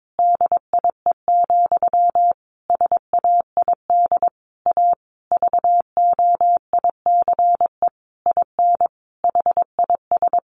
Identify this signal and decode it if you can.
Morse code, heard as a keyed tone, about 22 wpm, 700 Hz. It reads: DIE, SAID A VOICE IN HIS
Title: Sentences from Top 1300 Words 22wpm